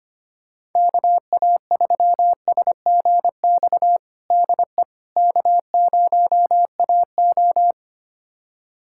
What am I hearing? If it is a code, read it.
Morse code, heard as a keyed tone, about 25 wpm, 700 Hz. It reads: KA3SGX DE K0AO
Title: USA Callsign DE Callsign 25wpm